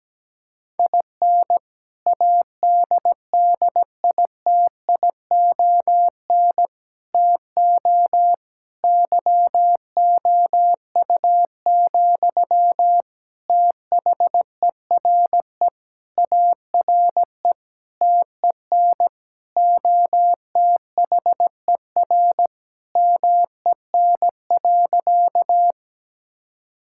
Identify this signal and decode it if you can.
Morse code, heard as a keyed tone, about 17 wpm, 700 Hz. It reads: IN ADDITION TO YOU, THERE ARE TEN OTHER MEN.
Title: Sentences from Top 1000 Words - Review Part 6 - 17wpm